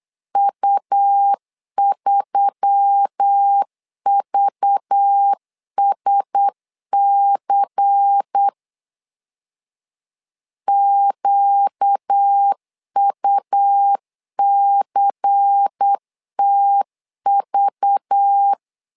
Morse code, heard as a keyed tone, about 8 wpm, 800 Hz. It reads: U3VSC QUCTV
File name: LEZIONI / 5 LBVC / 5 LBVC 8wpm 2935